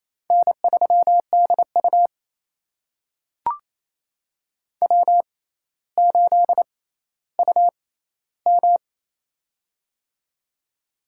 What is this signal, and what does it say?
Morse code, heard as a keyed tone, about 28 wpm, spaced as for 14 wpm, 700 Hz. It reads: N3DU W8UM
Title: USA Call Signs - 4 Characters - 6x Character Spacing - 28wpm